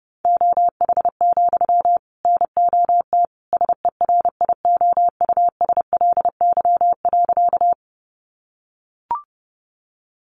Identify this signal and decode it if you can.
Morse code, heard as a keyed tone, about 30 wpm, 700 Hz. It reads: OH, NOT SERIOUSLY.
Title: Sentences from Top 1200 Words 30wpm